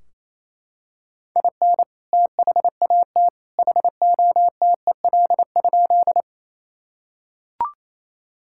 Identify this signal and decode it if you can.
Morse code, heard as a keyed tone, about 28 wpm, 700 Hz. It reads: IN THAT HOTEL?
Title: Sentences from Top 1600 Words 28wpm